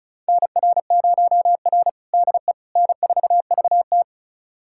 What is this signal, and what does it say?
Morse code, heard as a keyed tone, about 36 wpm, 700 Hz. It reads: NR0R DE N4VT
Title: USA Callsign DE Callsign - 4 Characters - 35wpm